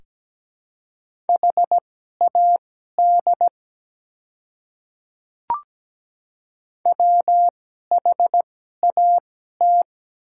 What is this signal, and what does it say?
Morse code, heard as a keyed tone, about 17 wpm, spaced as for 12 wpm, 700 Hz. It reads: HAD WHAT
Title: 2x Character Spacing - Top 100 Words - 17wpm